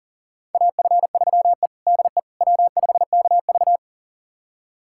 Morse code, heard as a keyed tone, about 40 wpm, 700 Hz. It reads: AF3E DE W5KV
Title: USA Callsign DE Callsign - 4 Characters - 40wpm